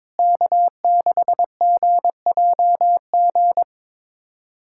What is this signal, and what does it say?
Morse code, heard as a keyed tone, about 22 wpm, 700 Hz. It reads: K6GJG
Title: USA Callsign DE Callsign - 5 Characters - 22wpm